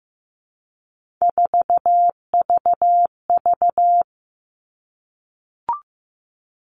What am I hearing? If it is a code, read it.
Morse code, heard as a keyed tone, about 15 wpm, 700 Hz. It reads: 4VV
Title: All Cut Numbers - 3-Digit 15wpm